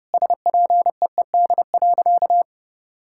Morse code, heard as a keyed tone, about 30 wpm, 700 Hz. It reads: SPEED.
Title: Sentences from Top 1000 Words - Review Part 5 - 30wpm